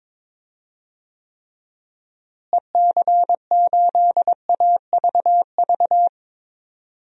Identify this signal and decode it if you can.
Morse code, heard as a keyed tone, about 22 wpm, 700 Hz. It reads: EC8AVV
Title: DXCC-100 Call Signs - 6 Characters - Repeated 3x - 22wpm